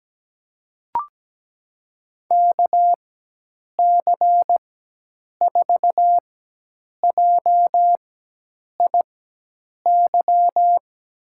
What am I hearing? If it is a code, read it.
Morse code, heard as a keyed tone, about 17 wpm, spaced as for 8 wpm, 700 Hz. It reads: KC4JIY
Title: USA Call Signs - 6 Characters - 4x Character Spacing - 17wpm